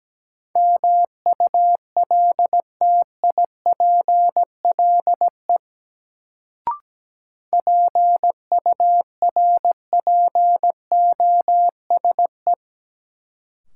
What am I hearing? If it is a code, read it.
Morse code, heard as a keyed tone, about 17 wpm, 700 Hz. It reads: MULTIPLE PURPOSE
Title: Sets of 5 Words - New Word Drill - 17wpm